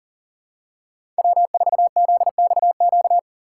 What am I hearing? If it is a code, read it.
Morse code, heard as a keyed tone, about 40 wpm, 700 Hz. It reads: W4ZXQ
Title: USA Call Signs - 6x Character Spacing - 40wpm